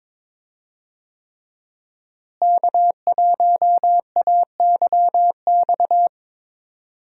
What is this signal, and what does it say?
Morse code, heard as a keyed tone, about 22 wpm, 700 Hz. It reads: K1AYX
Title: USA Call Signs - 5 Characters - Repeated 3x - 22wpm